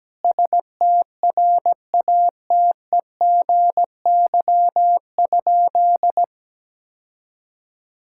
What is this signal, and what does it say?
Morse code, heard as a keyed tone, about 17 wpm, 700 Hz. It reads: STRATEGY?
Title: Sentences from Top 1000 Words - Review Part 5 - 17wpm